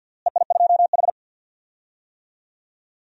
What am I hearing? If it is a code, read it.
Morse code, heard as a keyed tone, about 52 wpm, 700 Hz. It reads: EI2H
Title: DXCC-100 Call Signs - 4 Characters - Repeated 2x - 50wpm